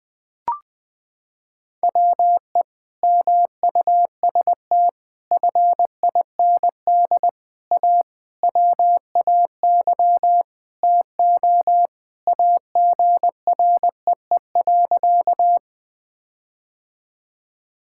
Morse code, heard as a keyed tone, about 21 wpm, 700 Hz. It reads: WE MUST FIND A WAY TO AGREE.
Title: Sentences with 5-Word Phrases - 20wpm